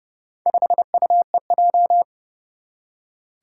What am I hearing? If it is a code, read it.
Morse code, heard as a keyed tone, about 30 wpm, 700 Hz. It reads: 5UEJ
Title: Four Challenging Letter-Number Combinations 30wpm